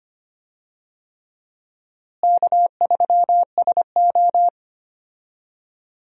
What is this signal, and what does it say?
Morse code, heard as a keyed tone, about 25 wpm, 700 Hz. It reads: K3SO
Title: USA Call Signs - 4 Characters - Repeated 3x - 25wpm